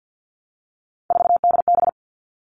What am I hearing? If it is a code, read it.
Morse code, heard as a keyed tone, about 48 wpm, 700 Hz. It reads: 4DB
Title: All Cut Numbers - 3-Digit 50wpm